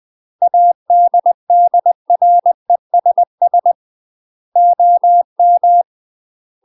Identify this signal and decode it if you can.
Morse code, heard as a keyed tone, about 20 wpm, 700 Hz. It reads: ADDRESS OM
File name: UK MASTER QSOs TRAINING / 20WPM / UK MASTER QSOs 0018